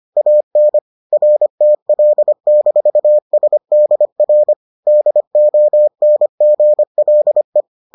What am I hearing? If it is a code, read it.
Morse code, heard as a keyed tone, about 25 wpm, 600 Hz. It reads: AN RTL-SDR DONGLE